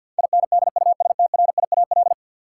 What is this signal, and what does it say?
Morse code, heard as a keyed tone, about 53 wpm, 700 Hz. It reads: INDUSTRIAL
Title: Sets of 5 Words - New Word Drill - 50wpm